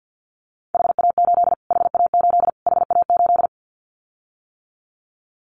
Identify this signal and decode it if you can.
Morse code, heard as a keyed tone, about 50 wpm, 700 Hz. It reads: HA8 HA8 HA8
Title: Callsign Prefixes for Top 25 DXCC - Repeated 3x - 50wpm